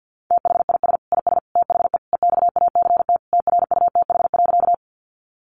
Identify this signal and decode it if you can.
Morse code, heard as a keyed tone, about 50 wpm, 700 Hz. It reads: THIS IS THE EXACT TRUTH.
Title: Sentences from Top 1300 Words 50wpm